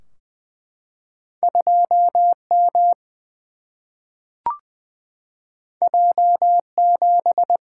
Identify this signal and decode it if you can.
Morse code, heard as a keyed tone, about 20 wpm, 700 Hz. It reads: 2 M J 7
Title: Single-Letter Single-Number 20wpm